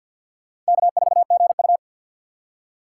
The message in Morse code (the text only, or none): K4GU